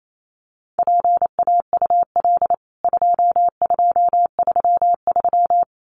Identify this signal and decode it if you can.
Morse code, heard as a keyed tone, about 28 wpm, 700 Hz. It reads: PAUL 2233
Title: CWT Exchange 28wpm